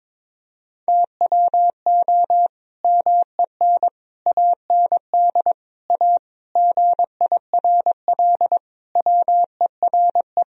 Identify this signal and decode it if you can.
Morse code, heard as a keyed tone, about 22 wpm, 700 Hz. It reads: TWO MEN AND A GIRL WERE
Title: Sentences from Top 1000 Words - Review Part 3 - 22wpm